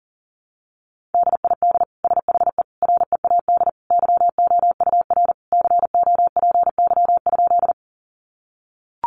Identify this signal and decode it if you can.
Morse code, heard as a keyed tone, about 40 wpm, 700 Hz. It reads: DID SHE READ YOUR COPY?
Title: Sentences from Top 1400 Words 40wpm